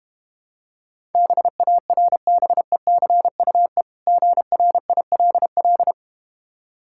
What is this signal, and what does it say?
Morse code, heard as a keyed tone, about 32 wpm, 700 Hz. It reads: BARBECUE GRILL